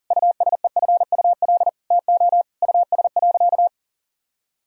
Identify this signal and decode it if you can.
Morse code, heard as a keyed tone, about 37 wpm, 700 Hz. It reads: USEFUL TO US.
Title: Sentences from Top 1000 Words - Review Part 4 - 40wpm